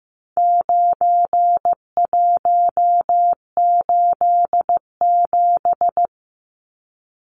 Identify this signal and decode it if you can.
Morse code, heard as a keyed tone, about 15 wpm, 700 Hz. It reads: 9187
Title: Numbers - 4-Digit 15wpm